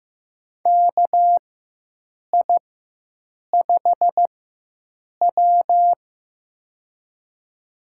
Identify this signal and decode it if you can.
Morse code, heard as a keyed tone, about 15 wpm, spaced as for 7 wpm, 700 Hz. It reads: KI5W
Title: USA Call Signs - 4 Characters - 4x Character Spacing - 15wpm